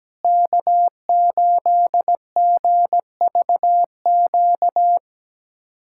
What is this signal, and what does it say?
Morse code, heard as a keyed tone, about 17 wpm, 700 Hz. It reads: K8GVQ
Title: USA Callsign DE Callsign - 5 Characters - 17wpm